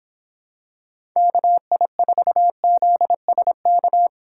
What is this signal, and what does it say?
Morse code, heard as a keyed tone, about 24 wpm, 700 Hz. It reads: KI4ZSK